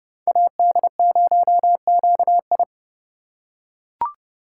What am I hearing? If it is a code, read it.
Morse code, heard as a keyed tone, about 30 wpm, 700 Hz. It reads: AD0QI
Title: USA Call Signs - 8x Character Spacing - 30wpm